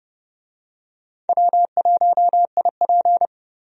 Morse code, heard as a keyed tone, about 30 wpm, 700 Hz. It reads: W1IP
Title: USA Call Signs - 4 Characters - 30wpm